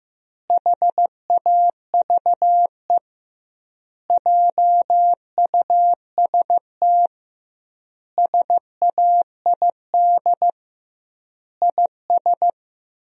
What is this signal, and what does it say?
Morse code, heard as a keyed tone, about 16 wpm, 700 Hz. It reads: HAVE JUST SAID IS
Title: 2x Word-Spacing - Sentences from Top 200 Words - 15wpm